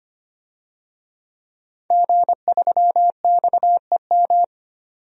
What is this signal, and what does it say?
Morse code, heard as a keyed tone, about 25 wpm, 700 Hz. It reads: G3XEM
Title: DXCC-100 Call Signs - 5 Characters - 25wpm